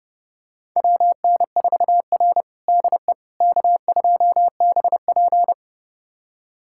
Morse code, heard as a keyed tone, about 31 wpm, 700 Hz. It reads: WN4R DE K2BP
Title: USA Callsign DE Callsign - 4 Characters - 30wpm